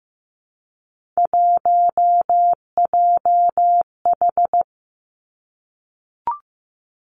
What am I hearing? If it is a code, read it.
Morse code, heard as a keyed tone, about 15 wpm, 700 Hz. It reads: 1JH
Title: Three Challenging Letter-Number Combinations 15wpm